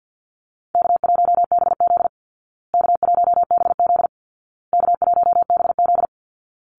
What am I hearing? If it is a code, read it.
Morse code, heard as a keyed tone, about 50 wpm, 700 Hz. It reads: K1BZ K1BZ K1BZ
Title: Call Signs Repeated 2x - 50wpm